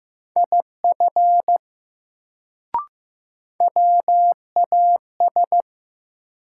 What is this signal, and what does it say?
Morse code, heard as a keyed tone, about 15 wpm, 700 Hz. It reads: IF WAS
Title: Sentences from Top 100 Words - New Word Drill - 15wpm